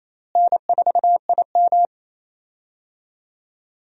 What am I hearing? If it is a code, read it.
Morse code, heard as a keyed tone, about 28 wpm, 700 Hz. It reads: N4IM
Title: USA Call Signs - 4 Characters - Repeated 2x - 28wpm